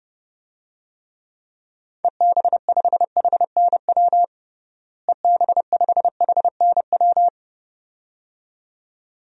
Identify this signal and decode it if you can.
Morse code, heard as a keyed tone, about 32 wpm, 700 Hz. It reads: EB5HNW EB5HNW
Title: DXCC-100 Call Signs - 6 Characters - Repeated 2x - 30wpm